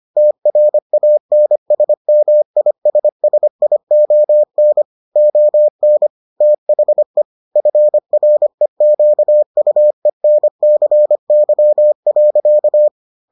Morse code, heard as a keyed tone, about 25 wpm, 600 Hz. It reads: TRANSMISSION ON THE FREQUENCY.